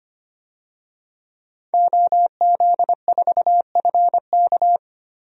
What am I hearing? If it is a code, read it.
Morse code, heard as a keyed tone, about 25 wpm, 700 Hz. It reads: OZ4FK
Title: DXCC-100 Call Signs - 5 Characters - Repeated 2x - 25wpm